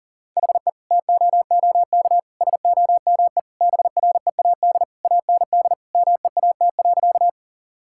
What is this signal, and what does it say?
Morse code, heard as a keyed tone, about 37 wpm, 700 Hz. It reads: HE TOOK SOME BREAD AND MEAT.
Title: Sentences from Top 1600 Words 40wpm